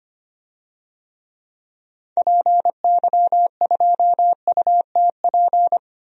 Morse code, heard as a keyed tone, about 25 wpm, 700 Hz. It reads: PY2UTP
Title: DXCC-100 Call Signs - 6 Characters - 25wpm